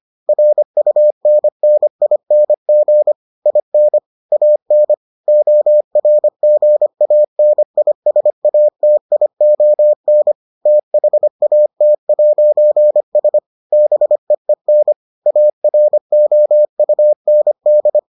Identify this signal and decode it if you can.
Morse code, heard as a keyed tone, about 25 wpm, 600 Hz. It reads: RUNNING IN AN ORGANISATION THAT'S BEEN AROUND